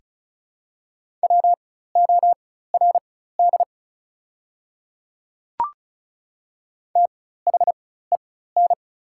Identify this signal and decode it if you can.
Morse code, heard as a keyed tone, about 33 wpm, spaced as for 16 wpm, 690 Hz. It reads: WORD THEN
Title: 4x Character Spacing - Top 100 Words - 35wpm